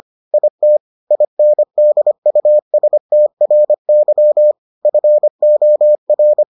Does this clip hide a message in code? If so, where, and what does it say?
Yes: Morse code from 0.3 to 6.4 s, heard as a keyed tone, about 25 wpm, 600 Hz. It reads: IT INDUSTRY FOR